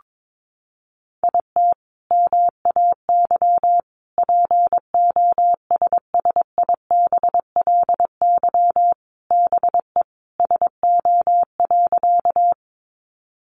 Morse code, heard as a keyed tone, about 22 wpm, 700 Hz. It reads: IT MAY POSSIBLY BE SO.
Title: Sentences from Top 1300 Words 22wpm